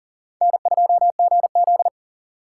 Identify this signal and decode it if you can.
Morse code, heard as a keyed tone, about 40 wpm, 700 Hz. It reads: N2GZ
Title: USA Call Signs - 4 Characters - 6x Character Spacing - 40wpm